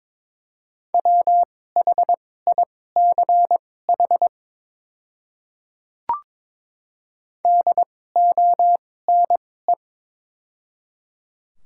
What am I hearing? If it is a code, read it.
Morse code, heard as a keyed tone, about 22 wpm, spaced as for 16 wpm, 700 Hz. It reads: WHICH DONE
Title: 2x Character Spacing - Top 500 Words - 22wpm